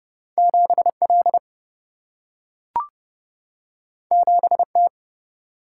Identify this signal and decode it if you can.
Morse code, heard as a keyed tone, about 30 wpm, 700 Hz. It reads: 7L 7T
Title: Single-Letter Single-Number 30wpm